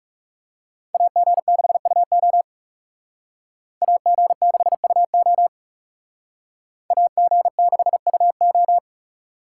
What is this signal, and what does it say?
Morse code, heard as a keyed tone, about 38 wpm, 700 Hz. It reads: AG6UO AG6UO AG6UO